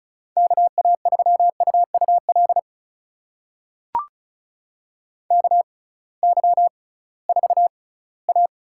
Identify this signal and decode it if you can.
Morse code, heard as a keyed tone, about 36 wpm, spaced as for 20 wpm, 700 Hz. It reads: KA3UUL KY4A
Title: USA Call Signs - 6x Character Spacing - 35wpm